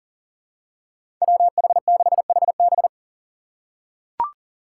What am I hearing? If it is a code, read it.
Morse code, heard as a keyed tone, about 40 wpm, 700 Hz. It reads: WH6HB
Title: USA Call Signs - 5 Characters - 6x Character Spacing - 40wpm